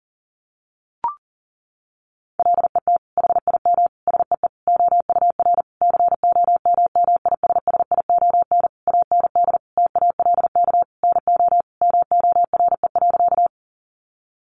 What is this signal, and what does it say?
Morse code, heard as a keyed tone, about 40 wpm, 700 Hz. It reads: LET HIM SEE OUR COMMISSION AND TALK NO MORE.
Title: Sentences from Top 1400 Words 40wpm